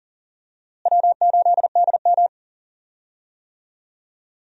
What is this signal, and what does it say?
Morse code, heard as a keyed tone, about 40 wpm, 700 Hz. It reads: W8DM
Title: USA Call Signs - 4 Characters - Repeated 3x - 40wpm